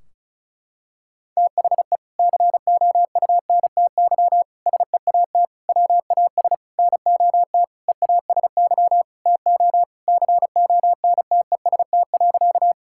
Morse code, heard as a keyed tone, about 36 wpm, 700 Hz. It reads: THE COUNTY SEAT WAS NOT EASY TO CONTEST.
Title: Sentences from Top 1600 Words 35wpm